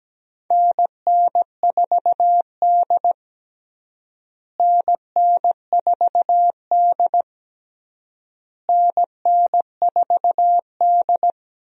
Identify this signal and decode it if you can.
Morse code, heard as a keyed tone, about 17 wpm, 700 Hz. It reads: NN4D NN4D NN4D
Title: USA Call Signs - 4 Characters - Repeated 3x - 17wpm